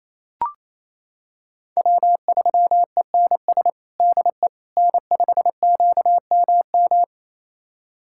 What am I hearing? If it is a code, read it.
Morse code, heard as a keyed tone, about 28 wpm, 700 Hz. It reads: W3ENS DE N5QMM
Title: USA Callsign DE Callsign - 5 Characters - 28wpm